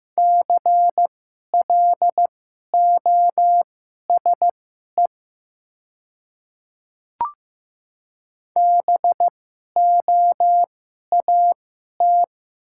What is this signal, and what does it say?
Morse code, heard as a keyed tone, about 15 wpm, spaced as for 11 wpm, 700 Hz. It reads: CLOSE BOAT
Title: 2x Character Spacing - Top 500 Words - 15wpm